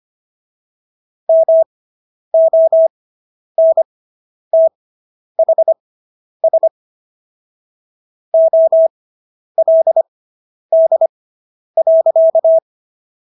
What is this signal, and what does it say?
Morse code, heard as a keyed tone, about 25 wpm, spaced as for 10 wpm, 650 Hz. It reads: MONTHS OLD.